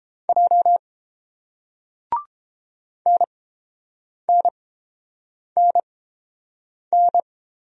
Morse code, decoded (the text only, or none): J NNNN